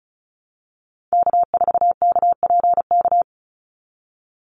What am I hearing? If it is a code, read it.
Morse code, heard as a keyed tone, about 34 wpm, 700 Hz. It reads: K4KPK
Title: NA SOTA Activator Callsigns 35wpm